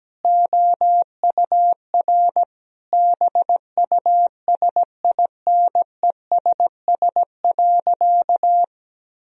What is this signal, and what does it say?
Morse code, heard as a keyed tone, about 17 wpm, 700 Hz. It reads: OUR BUSINESS.
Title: Sentences from Top 1000 Words - Review Part 4 - 17wpm